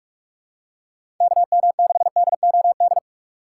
Morse code, heard as a keyed tone, about 44 wpm, 690 Hz. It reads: KM6DOD
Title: USA Call Signs - 4x Character Spacing - 45wpm